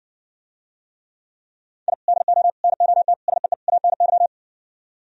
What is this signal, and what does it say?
Morse code, heard as a keyed tone, about 64 wpm, 700 Hz. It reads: I DO NOT SEE ANY